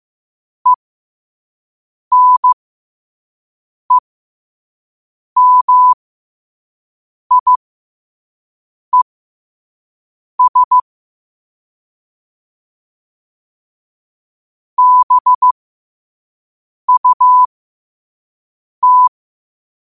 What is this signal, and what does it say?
Morse code, heard as a keyed tone, about 15 wpm, spaced as for 5 wpm, 1000 Hz. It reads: ENEMIES BUT
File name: qotdfiles / 5wpm